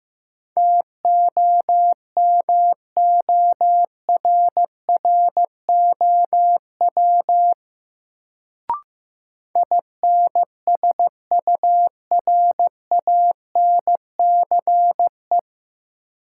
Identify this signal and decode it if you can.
Morse code, heard as a keyed tone, about 15 wpm, 700 Hz. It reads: TOMORROW INSURANCE
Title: Sentences from Top 1100 Words - New Word Drill - 15wpm